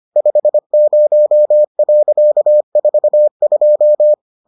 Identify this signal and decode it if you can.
Morse code, heard as a keyed tone, about 25 wpm, 600 Hz. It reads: 50.42